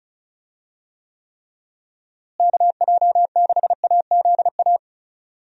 Morse code, heard as a keyed tone, about 36 wpm, 700 Hz. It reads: KJ6AZA
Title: USA Call Signs - 6 Characters - Repeated 3x - 35wpm